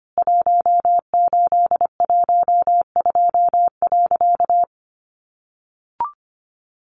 Morse code, heard as a keyed tone, about 25 wpm, 700 Hz. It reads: 1812.
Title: A Sound Of Thunder 25wpm